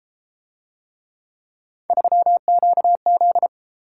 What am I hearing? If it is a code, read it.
Morse code, heard as a keyed tone, about 31 wpm, 700 Hz. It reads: 3QZ